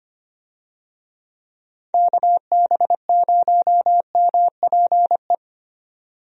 Morse code, heard as a keyed tone, about 25 wpm, 700 Hz. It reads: KB0MPE
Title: USA Call Signs - 6 Characters - 25wpm